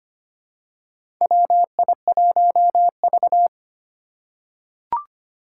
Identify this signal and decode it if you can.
Morse code, heard as a keyed tone, about 25 wpm, 700 Hz. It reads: WI1V
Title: USA Call Signs - 4 Characters - 3x Character Spacing - 25wpm